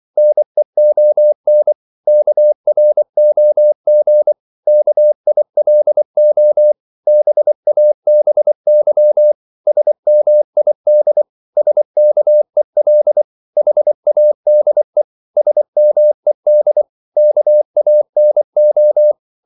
Morse code, heard as a keyed tone, about 24 wpm, 600 Hz. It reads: NEON KROG KILO BABY SMID SKEL HADE SMED KANO
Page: Korte danske ord 24wpm